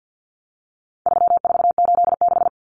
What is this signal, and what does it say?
Morse code, heard as a keyed tone, about 50 wpm, 700 Hz. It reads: 3486
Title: Common Cut Numbers - 4-Digit 50wpm